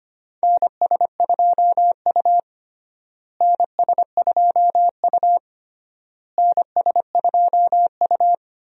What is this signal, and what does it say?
Morse code, heard as a keyed tone, about 25 wpm, 700 Hz. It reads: NS2U NS2U NS2U
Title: USA Call Signs - 4 Characters - Repeated 3x - 25wpm